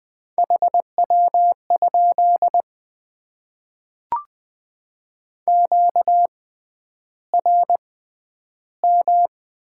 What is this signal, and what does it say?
Morse code, heard as a keyed tone, about 20 wpm, spaced as for 10 wpm, 700 Hz. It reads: HW? QRM
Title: Top 90 QSO Elements - 6x Character Spacing - 20wpm